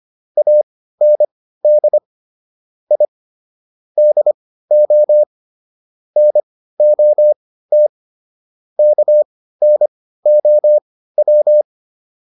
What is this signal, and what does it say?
Morse code, heard as a keyed tone, about 25 wpm, spaced as for 15 wpm, 600 Hz. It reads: AND I DO NOT KNOW